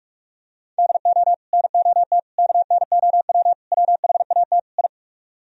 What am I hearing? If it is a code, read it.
Morse code, heard as a keyed tone, about 48 wpm, 700 Hz. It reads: DO NOT KNOW WHAT I